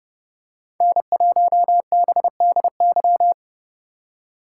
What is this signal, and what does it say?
Morse code, heard as a keyed tone, about 29 wpm, 700 Hz. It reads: N1BDY